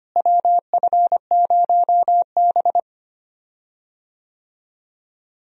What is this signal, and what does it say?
Morse code, heard as a keyed tone, about 25 wpm, 700 Hz. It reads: WF0B
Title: USA Call Signs - 4 Characters - Repeated 3x - 25wpm